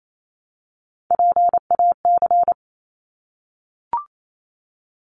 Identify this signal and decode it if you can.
Morse code, heard as a keyed tone, about 28 wpm, 700 Hz. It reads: PAC
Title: ARRL Sections 28wpm